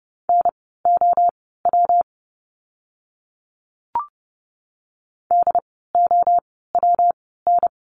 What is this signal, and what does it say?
Morse code, heard as a keyed tone, about 29 wpm, spaced as for 17 wpm, 700 Hz. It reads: NOW DOWN
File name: course / 30:10wpm / 115 Lesson 19f - Letter w - Words 30wpm